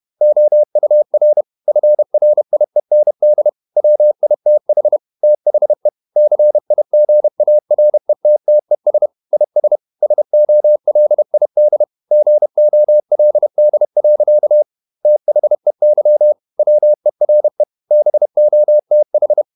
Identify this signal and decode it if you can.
Morse code, heard as a keyed tone, about 31 wpm, 600 Hz. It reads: OUR FRIEND WITH THE CIGARETTES IS SOLID GOLD. THEY WERE BOTH